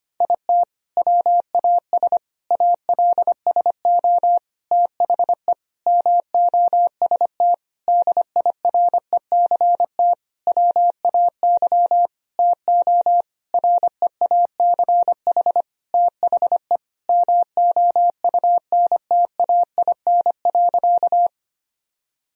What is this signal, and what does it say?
Morse code, heard as a keyed tone, about 25 wpm, 700 Hz. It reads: IT WAS ALSO THE MOST DIRECT WAY TO REACH THE MOUNTAIN.
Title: Sentences from Top 1000 Words - Review Part 2 - 25wpm